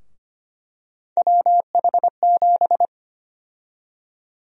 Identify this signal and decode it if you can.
Morse code, heard as a keyed tone, about 25 wpm, 700 Hz. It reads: WH7
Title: USA Call Sign Prefixes - Repeated 2x - 25wpm